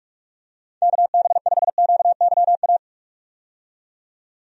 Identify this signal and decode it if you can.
Morse code, heard as a keyed tone, about 41 wpm, 700 Hz. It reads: KB5QYA
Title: USA Call Signs - 6 Characters - 45wpm